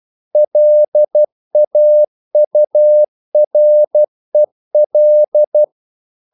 Morse code, heard as a keyed tone, about 12 wpm, 600 Hz. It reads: LAUREL
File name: mp3 files / the gardener - 12wpm